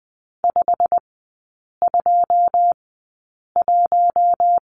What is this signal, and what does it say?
Morse code, heard as a keyed tone, about 20 wpm, spaced as for 8 wpm, 690 Hz. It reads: 521
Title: 087 Lesson 15e - Number 5 - Three Characters 20wpm